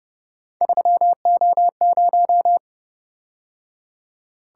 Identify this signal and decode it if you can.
Morse code, heard as a keyed tone, about 29 wpm, 700 Hz. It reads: 3O0
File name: Speed-Racing - Three Letter-Number Combinations - 25wpm